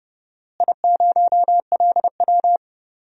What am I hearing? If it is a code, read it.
Morse code, heard as a keyed tone, about 30 wpm, 700 Hz. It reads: I0LW
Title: DXCC-100 Call Signs - 4 Characters - Repeated 3x - 30wpm